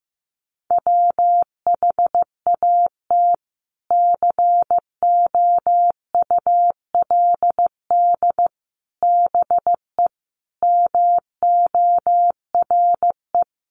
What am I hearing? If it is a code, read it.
Morse code, heard as a keyed tone, about 15 wpm, 700 Hz. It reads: WHAT COULD BE MORE
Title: Sentences from Top 1000 Words 15wpm